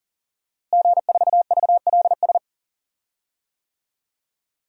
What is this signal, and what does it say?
Morse code, heard as a keyed tone, about 40 wpm, 700 Hz. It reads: G4VLS
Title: DXCC-100 Call Signs - 5 Characters - Repeated 2x - 40wpm